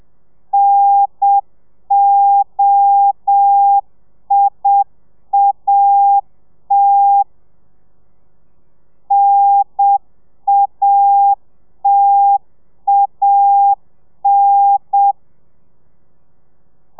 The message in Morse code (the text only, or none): NOIAT NATAN